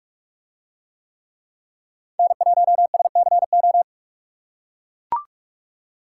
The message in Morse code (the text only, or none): N1SCO